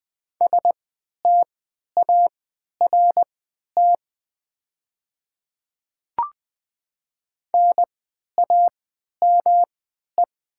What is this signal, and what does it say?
Morse code, heard as a keyed tone, about 20 wpm, spaced as for 11 wpm, 700 Hz. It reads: START NAME
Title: 3x Character Spacing - Top 300 Words - 20wpm